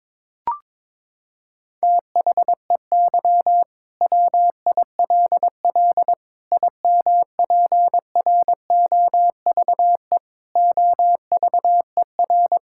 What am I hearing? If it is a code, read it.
Morse code, heard as a keyed tone, about 21 wpm, 700 Hz. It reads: THEY WILL IMPROVE OVER
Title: Sentences from Top 1000 Words - Review Part 5 - 22wpm